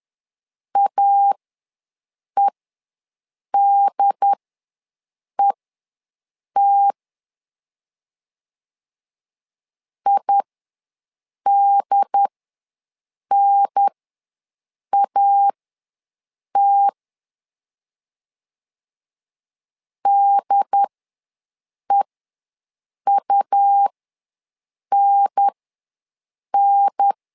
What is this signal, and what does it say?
Morse code, heard as a keyed tone, about 10 wpm, spaced as for 6 wpm, 800 Hz. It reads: AEDET IDNAT DEUNN
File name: 2 UHDN ws 10wpm